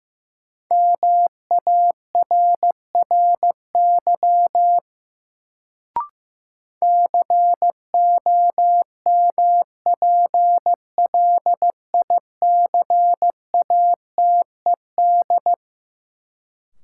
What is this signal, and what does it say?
Morse code, heard as a keyed tone, about 15 wpm, 700 Hz. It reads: MARRY COMPLICATED
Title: Sentences from Top 1200 Words - New Word Drill - 15wpm